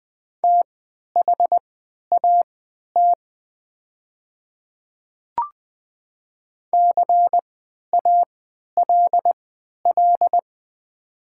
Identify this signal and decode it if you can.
Morse code, heard as a keyed tone, about 20 wpm, spaced as for 11 wpm, 700 Hz. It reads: THAT CALL
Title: 3x Character Spacing - Top 100 Words - 20wpm